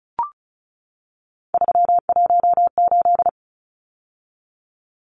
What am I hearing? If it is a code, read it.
Morse code, heard as a keyed tone, about 34 wpm, 700 Hz. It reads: 318
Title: Numbers - 3-Digit 35wpm